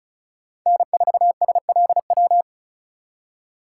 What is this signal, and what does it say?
Morse code, heard as a keyed tone, about 36 wpm, 700 Hz. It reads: N4SLW
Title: USA Call Signs - 5 Characters - 3x Character Spacing - 35wpm